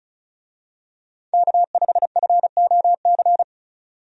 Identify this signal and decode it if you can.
Morse code, heard as a keyed tone, about 34 wpm, 700 Hz. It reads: K5FOC
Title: USA Call Signs - 5 Characters - 35wpm